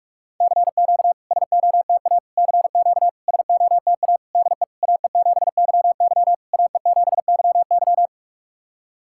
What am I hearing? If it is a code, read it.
Morse code, heard as a keyed tone, about 46 wpm, 700 Hz. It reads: CQ SOTA CQ SOTA DE AE7YY AE7YY
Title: SOTA QSO Practice 45wpm